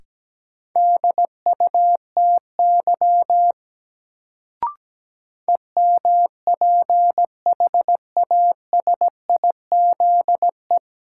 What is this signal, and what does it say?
Morse code, heard as a keyed tone, about 17 wpm, 700 Hz. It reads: DUTY EMPHASIZE
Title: Sentences from Top 1500 Words - New Word Drill Alt - 17wpm